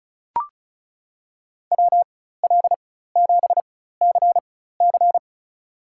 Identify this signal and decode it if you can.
Morse code, heard as a keyed tone, about 34 wpm, spaced as for 16 wpm, 700 Hz. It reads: WL7CC
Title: USA Call Signs - 5 Characters - 4x Character Spacing - 35wpm